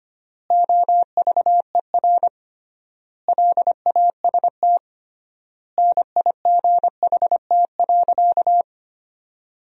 Morse code, heard as a keyed tone, about 25 wpm, 700 Hz. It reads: OVER LAST NIGHT.
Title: 3x Word-Spacing - Sentences from Top 300 Words - 25wpm